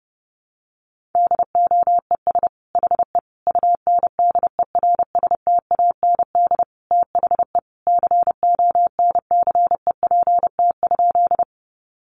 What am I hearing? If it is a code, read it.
Morse code, heard as a keyed tone, about 30 wpm, 700 Hz. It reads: DOES HE UNDERSTAND THE CONCEPT?
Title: Sentences from Top 1100 Words 30wpm